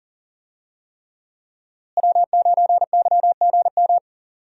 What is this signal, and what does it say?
Morse code, heard as a keyed tone, about 40 wpm, 700 Hz. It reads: W9YGM
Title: USA Call Signs - 5 Characters - Repeated 3x - 40wpm